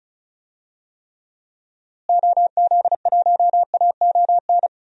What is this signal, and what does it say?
Morse code, heard as a keyed tone, about 34 wpm, 700 Hz. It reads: OZ1AON
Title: DXCC-100 Call Signs - 6 Characters - Repeated 3x - 35wpm